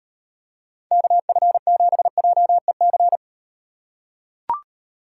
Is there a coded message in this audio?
Yes: Morse code from 0.9 to 3.2 s, heard as a keyed tone, about 38 wpm, 700 Hz. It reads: KF7JEC